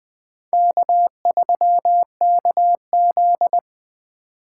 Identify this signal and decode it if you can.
Morse code, heard as a keyed tone, about 24 wpm, 700 Hz. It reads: K3KZ
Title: Speed Racing - USA Call Signs 20wpm